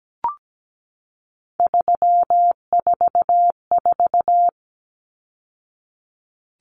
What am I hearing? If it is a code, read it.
Morse code, heard as a keyed tone, about 17 wpm, 700 Hz. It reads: 344
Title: Numbers - 3-Digit 17wpm